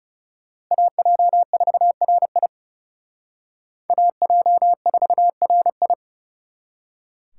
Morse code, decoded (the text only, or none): AJ4RI AJ4RI